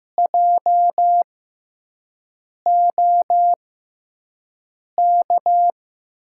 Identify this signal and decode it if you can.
Morse code, heard as a keyed tone, about 15 wpm, spaced as for 5 wpm, 700 Hz. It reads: JOK
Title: USA Call Signs - 5 Characters - 6x Character Spacing - 15wpm